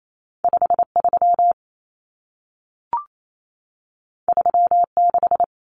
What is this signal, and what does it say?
Morse code, heard as a keyed tone, about 28 wpm, 700 Hz. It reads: 53 36
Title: Common Cut Numbers - 2-Digit 28wpm